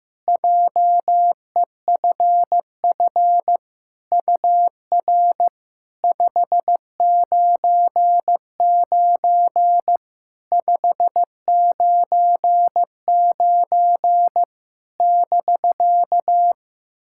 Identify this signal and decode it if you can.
Morse code, heard as a keyed tone, about 15 wpm, 700 Hz. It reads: JEFF UR 599 599 <BK>
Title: SOTA QSO Practice 15wpm